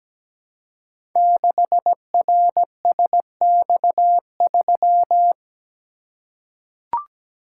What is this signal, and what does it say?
Morse code, heard as a keyed tone, about 17 wpm, 700 Hz. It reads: 6RSX3
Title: Code Groups - 5 Letter:Numbers - 17wpm